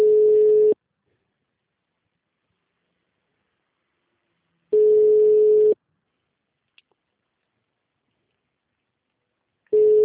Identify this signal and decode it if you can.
Ringback tone (Europe, 425 Hz).
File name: out- -2008- - - .34280